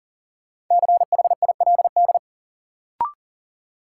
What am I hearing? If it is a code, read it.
Morse code, heard as a keyed tone, about 40 wpm, 700 Hz. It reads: CHILD
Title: Sets of 5 Words Encore - New Word Drill - 40wpm